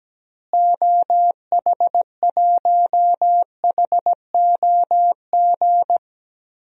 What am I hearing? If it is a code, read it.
Morse code, heard as a keyed tone, about 17 wpm, 700 Hz. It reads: OH1HOG
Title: DXCC-100 Call Signs - 6 Characters - Repeated 2x - 17wpm